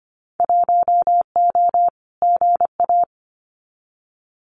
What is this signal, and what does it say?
Morse code, heard as a keyed tone, about 25 wpm, 700 Hz. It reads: 1O GA
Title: Winter Field Day Exchanges 25wpm